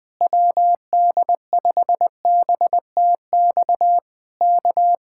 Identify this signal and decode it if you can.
Morse code, heard as a keyed tone, about 20 wpm, 700 Hz. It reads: WD5BTX K
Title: Speed-Racing Common QSO Phrases - 17wpm